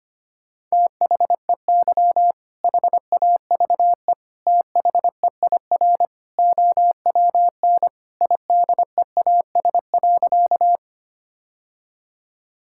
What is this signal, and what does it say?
Morse code, heard as a keyed tone, about 25 wpm, 700 Hz. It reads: THEY HAVE THEIR OWN IDEAS.
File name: practice / 25wpm / Sentences with 2-Word Phrases - 25wpm